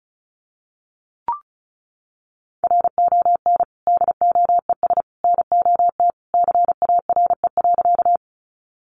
Morse code, heard as a keyed tone, about 34 wpm, 700 Hz. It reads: RON DOES NOT CARE.
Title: Additional Easy Phrases 35wpm